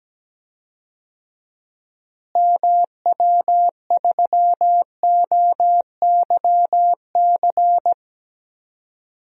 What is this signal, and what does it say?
Morse code, heard as a keyed tone, about 17 wpm, 700 Hz. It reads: MW3OYC
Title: DXCC-100 Call Signs - 6 Characters - Repeated 2x - 17wpm